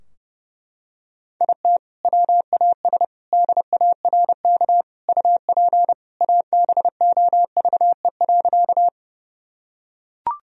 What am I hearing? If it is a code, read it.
Morse code, heard as a keyed tone, about 30 wpm, 700 Hz. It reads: IT WAS DARK UP ABOVE.
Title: Sentences from Top 1000 Words - Review Part 2 - 30wpm